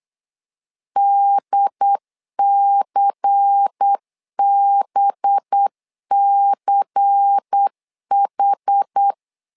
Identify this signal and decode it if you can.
Morse code, heard as a keyed tone, about 8 wpm, 800 Hz. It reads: DCBCH
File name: LEZIONI / 5 LBVC / 5 LBVC 8wpm 2935